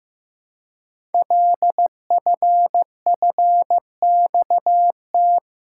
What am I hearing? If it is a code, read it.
Morse code, heard as a keyed tone, about 15 wpm, 700 Hz. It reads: LFFXT
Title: Code Groups - 5 Letters - 15wpm